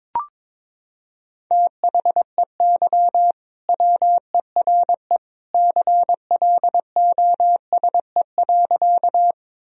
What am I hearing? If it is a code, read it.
Morse code, heard as a keyed tone, about 22 wpm, 700 Hz. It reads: THEY WERE CLOSE.
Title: Sentences from Top 1000 Words - Review Part 1 - 22wpm